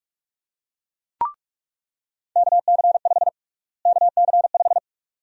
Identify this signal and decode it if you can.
Morse code, heard as a keyed tone, about 46 wpm, 700 Hz. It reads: KC5 KC5
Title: USA Call Sign Prefixes - Repeated 2x - 45wpm